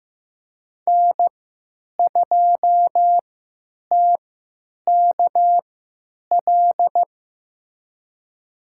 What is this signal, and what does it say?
Morse code, heard as a keyed tone, about 15 wpm, spaced as for 9 wpm, 700 Hz. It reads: N2TKL
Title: USA Call Signs - 5 Characters - 3x Character Spacing - 15wpm